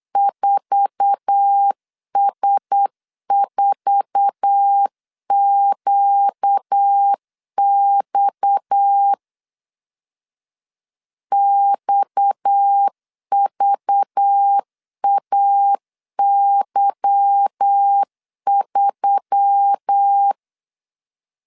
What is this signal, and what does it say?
Morse code, heard as a keyed tone, about 8 wpm, 800 Hz. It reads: 4S4QX XVAY3
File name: LEZIONI / 8 YJ45 / 8 YJ4 8wpm 3246